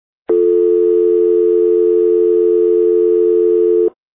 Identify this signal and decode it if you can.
Dial tone (North America), 350 + 440 Hz.
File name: music / dialtone